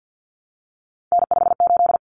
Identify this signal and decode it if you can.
Morse code, heard as a keyed tone, about 50 wpm, 700 Hz. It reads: N58
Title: Common Cut Numbers - 3-Digit 50wpm